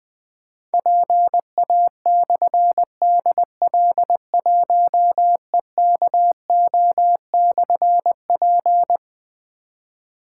Morse code, heard as a keyed tone, about 20 wpm, 700 Hz. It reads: PA/DL1EKO/P
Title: Difficult Call Signs 20wpm